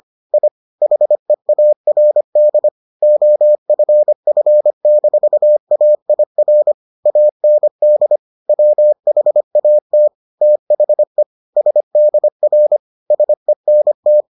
Morse code, heard as a keyed tone, about 25 wpm, 600 Hz. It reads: I HEARD OFF-AIR AND WHAT THE SDR SENT